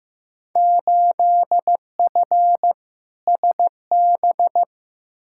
Morse code, heard as a keyed tone, about 15 wpm, 700 Hz. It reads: 8F SB
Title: Field Day Exchanges Encore - 15wpm